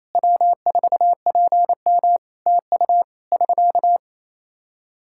Morse code, heard as a keyed tone, about 28 wpm, 700 Hz. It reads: W4PM TU <SK>
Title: Practice QSO - Exchange 4 - Ending - 28wpm